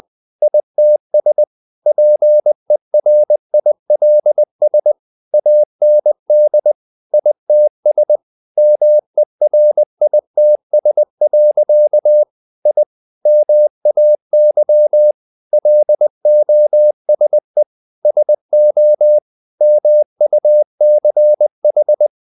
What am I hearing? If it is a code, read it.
Morse code, heard as a keyed tone, about 20 wpm, 600 Hz. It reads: ITS PERILS AND ITS MERITS. I MAY LOSE SO MUCH
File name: the old man and the sea part 2 20wpm